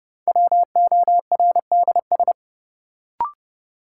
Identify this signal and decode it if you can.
Morse code, heard as a keyed tone, about 30 wpm, 700 Hz. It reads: WORDS
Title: Sentences from Top 1000 Words - Review Part 6 - New Word Drill - 30wpm